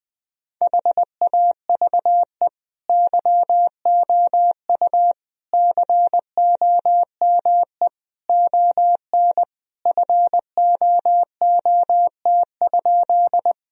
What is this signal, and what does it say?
Morse code, heard as a keyed tone, about 21 wpm, 700 Hz. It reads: HAVE YOU COME ON FOOT?
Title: Sentences from Top 1000 Words - Review Part 3 - 20wpm